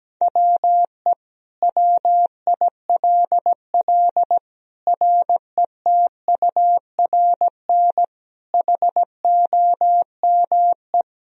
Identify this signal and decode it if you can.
Morse code, heard as a keyed tone, about 17 wpm, 700 Hz. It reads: WE WILL RETURN HOME
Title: Sentences from Top 1000 Words - Review Part 3 - 17wpm